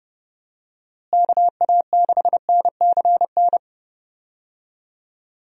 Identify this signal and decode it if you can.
Morse code, heard as a keyed tone, about 30 wpm, 700 Hz. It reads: KA6NCN
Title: USA Call Signs - 6 Characters - Repeated 2x - 30wpm